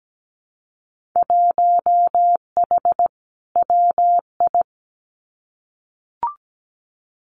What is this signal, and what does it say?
Morse code, heard as a keyed tone, about 17 wpm, 690 Hz. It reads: 1H WI
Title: Winter Field Day Exchanges 17wpm